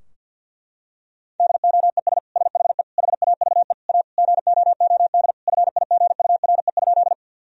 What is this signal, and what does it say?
Morse code, heard as a keyed tone, about 50 wpm, 700 Hz. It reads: DOES SHE HAVE A GOOD FIGURE?
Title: Sentences from Top 1000 Words - Review Part 2 - 50wpm